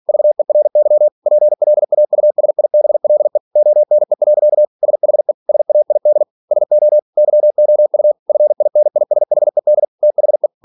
Morse code, heard as a keyed tone, about 46 wpm, 600 Hz. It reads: VERY PLAUSIBLE ONE. SHE SAID SO YOU FINISHED THE
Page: Never 47wpm